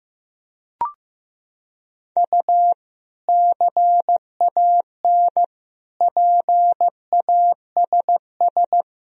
Morse code, heard as a keyed tone, about 16 wpm, 700 Hz. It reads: U CAN PASS
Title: Sentences from Top 400 Words and Abbreviations 15wpm